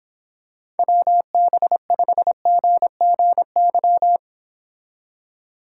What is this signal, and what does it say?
Morse code, heard as a keyed tone, about 26 wpm, 700 Hz. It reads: WB5GGY